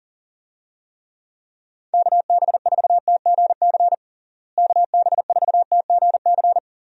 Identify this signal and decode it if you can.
Morse code, heard as a keyed tone, about 37 wpm, 700 Hz. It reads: KB4TGC KB4TGC
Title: USA Call Signs - 6 Characters - Repeated 2x - 40wpm